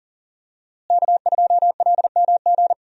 Morse code, heard as a keyed tone, about 40 wpm, 700 Hz. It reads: K2LMG
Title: USA Call Signs - 5 Characters - Repeated 3x - 40wpm